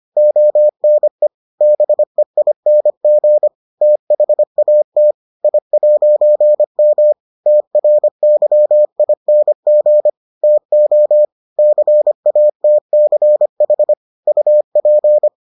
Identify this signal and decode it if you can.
Morse code, heard as a keyed tone, about 25 wpm, 600 Hz. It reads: ONE BEING THAT I'M TRYING TO CATCH UP